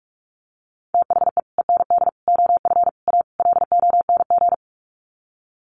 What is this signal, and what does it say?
Morse code, heard as a keyed tone, about 44 wpm, 700 Hz. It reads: THE END OF A LONG
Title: Sets of 5 Words Encore2 45wpm